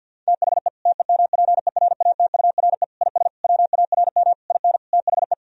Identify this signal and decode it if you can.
Morse code, heard as a keyed tone, about 52 wpm, 700 Hz. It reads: THE TEMPERATURE IS WARM IN THE
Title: Sentences from Top 1000 Words - Review Part 4 - 50wpm